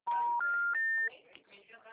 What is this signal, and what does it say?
Special information tone, with speech.